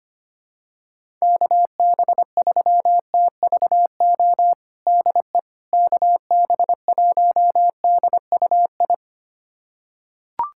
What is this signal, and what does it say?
Morse code, heard as a keyed tone, about 25 wpm, 700 Hz. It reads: KB3TVO DE KB1DUI
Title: USA Callsign DE Callsign - 6 Characters - 25wpm